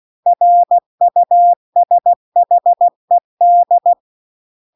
Morse code, heard as a keyed tone, about 16 wpm, 700 Hz. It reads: RUSHED
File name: CW-The-Big-Twoheaded-River-2-16WPM